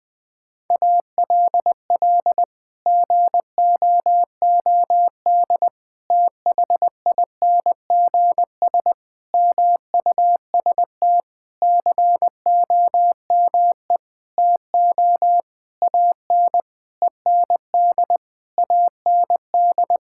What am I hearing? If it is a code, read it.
Morse code, heard as a keyed tone, about 20 wpm, 700 Hz. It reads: ALL GOOD THINGS MUST COME TO AN END AND
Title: Sentences with 9-Word Phrases - Spoken Beginning - 20wpm